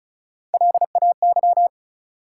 Morse code, heard as a keyed tone, about 36 wpm, 700 Hz. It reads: LAY
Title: Three Letter Words - 35wpm